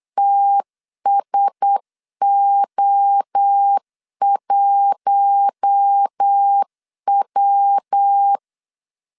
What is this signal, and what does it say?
Morse code, heard as a keyed tone, about 8 wpm, 800 Hz. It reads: TSO1W
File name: LEZIONI / 7 GF9Z / 7 G 8wpm 3118